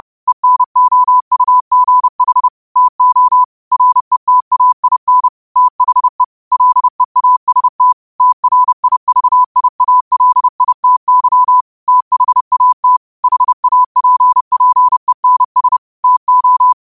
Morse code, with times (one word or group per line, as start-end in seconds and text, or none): 0.3-2.5 ENOUGH
2.8-3.4 TO
3.7-5.3 RETAIN
5.6-6.2 THE
6.5-7.9 LEAST
8.2-11.6 TRIVIALITY
11.9-13.0 THAT
13.2-15.8 HAPPENS
16.0-16.7 TO